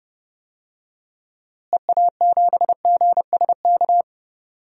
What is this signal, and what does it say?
Morse code, heard as a keyed tone, about 30 wpm, 700 Hz. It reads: EA7GSK
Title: DXCC-100 Call Signs - 6 Characters - Repeated 3x - 30wpm